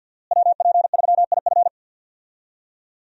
Morse code, heard as a keyed tone, about 52 wpm, 700 Hz. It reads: WP3IF
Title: USA Call Signs - 5 Characters - Repeated 2x - 50wpm